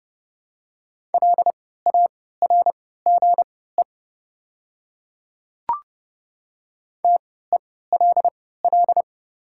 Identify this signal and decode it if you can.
Morse code, heard as a keyed tone, about 29 wpm, spaced as for 17 wpm, 700 Hz. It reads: LARGE TELL
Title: 3x Character Spacing - Top 300 Words - 30wpm